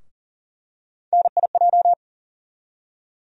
Morse code, heard as a keyed tone, about 40 wpm, 700 Hz. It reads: NIJ
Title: USA Call Sign Postfixes - 3 Letters 40wpm